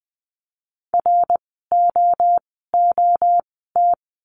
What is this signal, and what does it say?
Morse code, heard as a keyed tone, about 20 wpm, spaced as for 14 wpm, 700 Hz. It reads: ROOT
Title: 068 Lesson 12f - Letter d - Words 20wpm